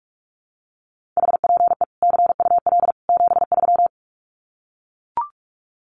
Morse code, heard as a keyed tone, about 44 wpm, 700 Hz. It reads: HPE CUL 73
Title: Common QSO Phrases 45wpm